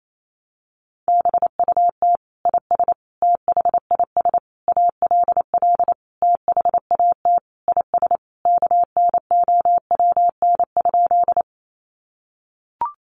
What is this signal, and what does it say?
Morse code, heard as a keyed tone, about 28 wpm, 700 Hz. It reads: BUT IS THIS ALL THAT IS KNOWN?
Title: Sentences from Top 600 Words 28wpm